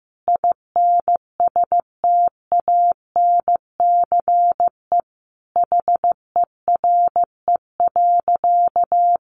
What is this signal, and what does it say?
Morse code, heard as a keyed tone, about 15 wpm, 700 Hz. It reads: INSTANCE HERE.
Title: Sentences from Top 800 Words 15wpm